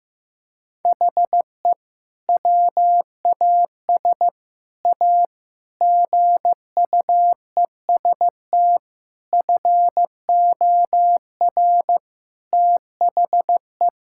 Morse code, heard as a keyed tone, about 15 wpm, 700 Hz. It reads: HE WAS A GUEST FOR THE
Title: Sentences from Top 1600 Words 15wpm